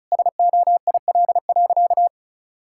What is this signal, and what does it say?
Morse code, heard as a keyed tone, about 36 wpm, 700 Hz. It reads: SOIL.
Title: Sentences from Top 1000 Words - Review Part 5 - 35wpm